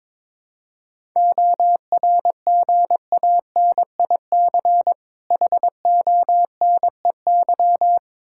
Morse code, heard as a keyed tone, about 22 wpm, 700 Hz. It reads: ORGANIC HONEY
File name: Difficult Words with Hints - 22wpm